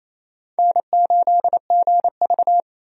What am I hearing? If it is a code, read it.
Morse code, heard as a keyed tone, about 28 wpm, 700 Hz. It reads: N8GV
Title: USA Call Signs - 4 Characters - Repeated 2x - 28wpm